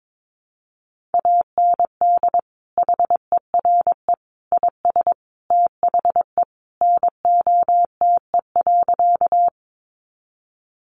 Morse code, heard as a keyed tone, about 22 wpm, 700 Hz. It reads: AND HERE IS THE NOTE.
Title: Sentences from Top 500 Words 22wpm